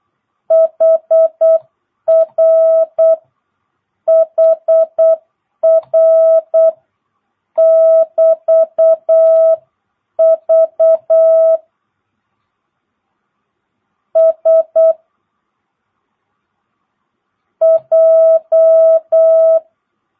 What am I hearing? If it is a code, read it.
Morse code, heard as a keyed tone, about 8 wpm, 1300 Hz, with noise. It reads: HR HR <BT> V S J
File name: jh8cbh / 2022cw